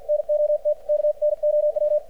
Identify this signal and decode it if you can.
Morse code, heard as a keyed tone, about 53 wpm, 600 Hz, with radio noise. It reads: NOT KNOW
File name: archived / CRSnet